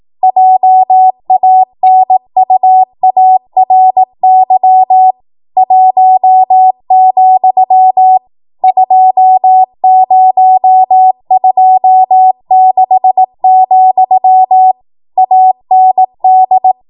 Morse code, heard as a keyed tone, about 18 wpm, 750 Hz. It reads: JANUARY 1, 2026, AND
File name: Morse / 18WPM